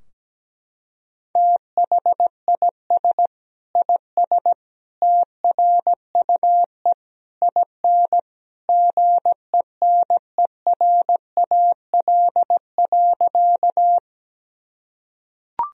Morse code, heard as a keyed tone, about 17 wpm, 700 Hz. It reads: THIS IS TRUE IN GENERAL.
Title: Sentences from Top 1000 Words - Review Part 3 - 17wpm